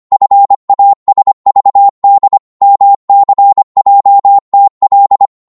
Morse code, heard as a keyed tone, about 25 wpm, 800 Hz. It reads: FASVD MCJTL